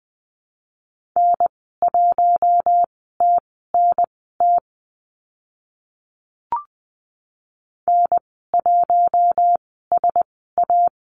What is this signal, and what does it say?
Morse code, heard as a keyed tone, about 20 wpm, spaced as for 14 wpm, 700 Hz. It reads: N1TNT N1SA
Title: 062 Lesson 11g - Letter h - Call Signs 20wpm